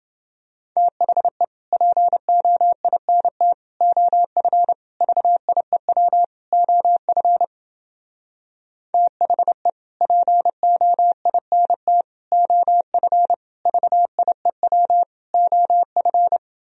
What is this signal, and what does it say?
Morse code, heard as a keyed tone, about 29 wpm, 700 Hz. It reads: THE POINT OF VIEW OF THE POINT OF VIEW OF